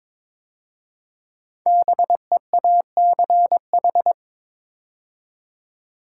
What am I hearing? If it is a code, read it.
Morse code, heard as a keyed tone, about 22 wpm, 700 Hz. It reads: BEACH
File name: 5 Letter Words - First 3 Letters Given - 22wpm